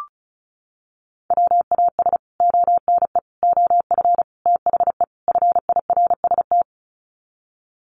Morse code, heard as a keyed tone, about 34 wpm, 700 Hz. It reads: WAS ONE OF THE FIRST
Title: Sets of 5 Words 35wpm